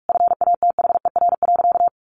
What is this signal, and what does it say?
Morse code, heard as a keyed tone, about 44 wpm, 700 Hz. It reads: FATHER.
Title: Sentences from Top 1300 Words 45wpm